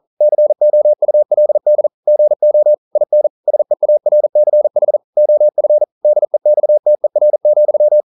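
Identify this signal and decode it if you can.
Morse code, heard as a keyed tone, about 40 wpm, 600 Hz. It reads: COULD GO IN SEARCH OF DEXTER,
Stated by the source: Morse code practice at 41 wpm, from CW med Gnister.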